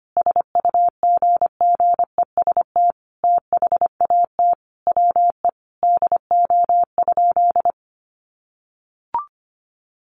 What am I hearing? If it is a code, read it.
Morse code, heard as a keyed tone, about 25 wpm, 700 Hz. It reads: SUGGEST THAT WE DO?
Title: Sentences from Top 1300 Words 25wpm